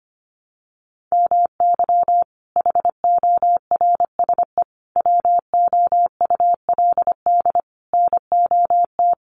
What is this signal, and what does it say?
Morse code, heard as a keyed tone, about 25 wpm, 700 Hz. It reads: MY HORSE WOULD NOT
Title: Sentences from Top 1400 Words 25wpm